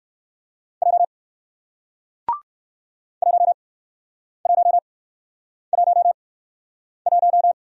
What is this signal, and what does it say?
Morse code, heard as a keyed tone, about 61 wpm, spaced as for 9 wpm, 700 Hz. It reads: J 1111